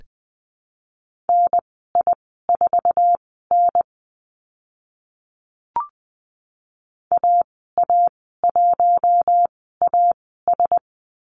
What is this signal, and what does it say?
Morse code, heard as a keyed tone, about 20 wpm, spaced as for 14 wpm, 700 Hz. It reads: NI4N AA1AS
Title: 082 Lesson 14g - Number 2 - Call Signs 20wpm